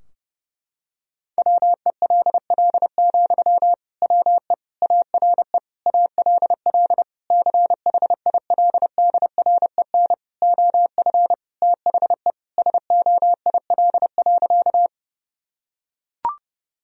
Morse code, heard as a keyed tone, about 28 wpm, 700 Hz. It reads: WELL, WE ARE ALL CHILDREN OF THE SOIL.
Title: Sentences from Top 1000 Words - Review Part 4 - 30wpm